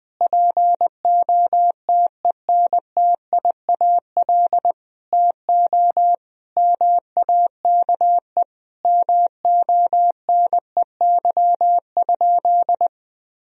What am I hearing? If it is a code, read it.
Morse code, heard as a keyed tone, about 20 wpm, 700 Hz. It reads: POTENTIAL TO MAKE MONEY?
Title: Sentences from Top 1000 Words - Review Part 4 - 20wpm